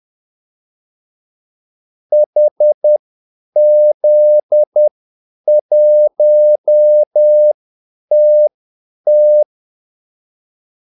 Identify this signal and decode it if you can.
Morse code, heard as a keyed tone, about 10 wpm, spaced as for 8 wpm, 600 Hz. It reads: HZ1TT
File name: Callsigns-10-8wpm